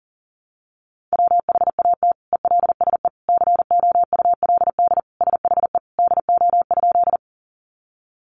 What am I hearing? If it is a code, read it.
Morse code, heard as a keyed tone, about 40 wpm, 700 Hz. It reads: WHAT ELSE COULD SHE DO?
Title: Sentences from Top 800 Words 40wpm